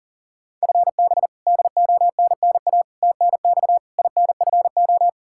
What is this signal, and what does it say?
Morse code, heard as a keyed tone, about 37 wpm, 700 Hz. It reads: FB DONNA TNX INFO
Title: Practice QSO - Exchange 3 - More Info - 40wpm